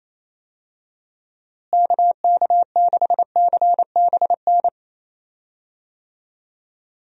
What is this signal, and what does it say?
Morse code, heard as a keyed tone, about 27 wpm, 700 Hz. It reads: KK6CBN
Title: USA Call Signs - 6 Characters - Repeated 2x - 28wpm